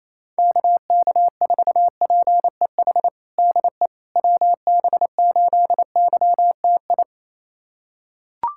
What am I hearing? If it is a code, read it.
Morse code, heard as a keyed tone, about 27 wpm, 700 Hz. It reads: KK4PEH DE WB8YTI
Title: USA Callsign DE Callsign - 6 Characters - 28wpm